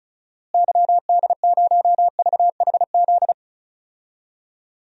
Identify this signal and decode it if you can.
Morse code, heard as a keyed tone, about 36 wpm, 700 Hz. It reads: YD0VHZ
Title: DXCC-100 Call Signs - 6 Characters - Repeated 2x - 35wpm